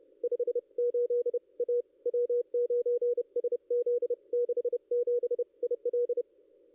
Morse code, heard as a keyed tone, about 30 wpm, 440 Hz. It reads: 58 A W9SZ 67 IL
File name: contests / ss-cw-k5tr-op-n5rz -2200